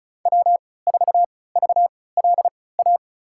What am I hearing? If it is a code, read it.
Morse code, heard as a keyed tone, about 34 wpm, spaced as for 20 wpm, 700 Hz. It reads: W4VLA
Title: USA Call Signs - 5 Characters - 3x Character Spacing - 35wpm